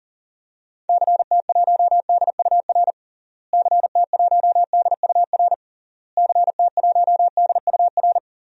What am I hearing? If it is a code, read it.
Morse code, heard as a keyed tone, about 40 wpm, 700 Hz. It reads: CT1DUR CT1DUR CT1DUR
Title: DXCC-100 Call Signs - 6 Characters - Repeated 3x - 40wpm